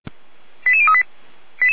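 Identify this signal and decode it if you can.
Five-tone selective call (ZVEI), heard as quick runs of five tones, tones of about 70 ms. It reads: 88038
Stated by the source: ZVEI five-tone selective call received by SDR, from feuerwehr-orlamuende.org fire brigade audio archive.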